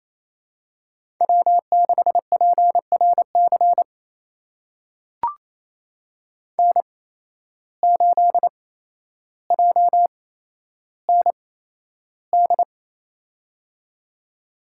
Morse code, heard as a keyed tone, about 28 wpm, spaced as for 12 wpm, 700 Hz. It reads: W6PRC N8JND
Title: USA Call Signs - 5 Characters - 8x Character Spacing - 28wpm